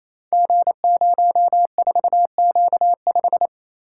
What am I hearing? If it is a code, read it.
Morse code, heard as a keyed tone, about 28 wpm, 700 Hz. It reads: G04Q5
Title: Code Groups - 5 Letter:Numbers - 28wpm